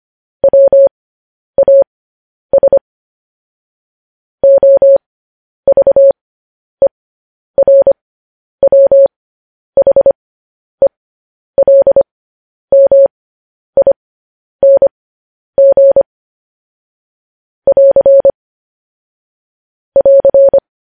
Morse code, heard as a keyed tone, about 25 wpm, spaced as for 10 wpm, 550 Hz. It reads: WAS OVERWHELMING <AR> <AR>
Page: news-headlines-in-morse-code-at-10-wpm